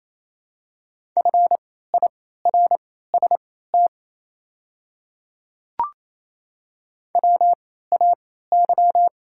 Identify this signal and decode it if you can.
Morse code, heard as a keyed tone, about 28 wpm, spaced as for 16 wpm, 700 Hz. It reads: FIRST WAY
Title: 3x Character Spacing - Top 300 Words - 28wpm